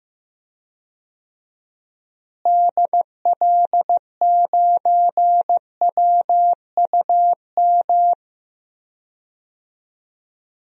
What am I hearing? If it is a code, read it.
Morse code, heard as a keyed tone, about 15 wpm, 700 Hz. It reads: DL9WUM
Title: DXCC-100 Call Signs - 6 Characters - Repeated 2x - 15wpm